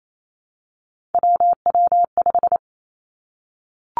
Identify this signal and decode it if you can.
Morse code, heard as a keyed tone, about 28 wpm, 700 Hz. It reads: WW5
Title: Three Challenging Letter-Number Combinations 28wpm